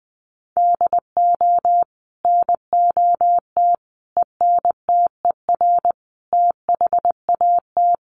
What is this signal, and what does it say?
Morse code, heard as a keyed tone, about 20 wpm, 700 Hz. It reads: DO NOT ENTER THAT
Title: Sentences from Top 1300 Words 20wpm